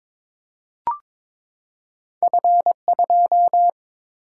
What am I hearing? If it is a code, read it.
Morse code, heard as a keyed tone, about 22 wpm, 690 Hz. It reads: F2
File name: practice / 22wpm / Two Letter-Number Combinations - 22wpm